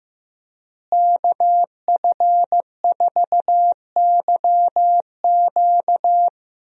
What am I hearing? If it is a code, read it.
Morse code, heard as a keyed tone, about 15 wpm, 700 Hz. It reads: KF4YQ
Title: USA Call Signs - 5 Characters - Repeated 2x - 15wpm